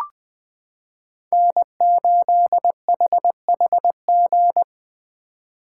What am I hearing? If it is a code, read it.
Morse code, heard as a keyed tone, about 20 wpm, 700 Hz. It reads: N8HHG
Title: K1USN SST QSO Practice - 20wpm